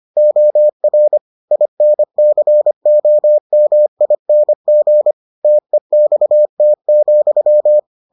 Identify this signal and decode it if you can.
Morse code, heard as a keyed tone, about 25 wpm, 600 Hz. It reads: OR INCOMING TEXT,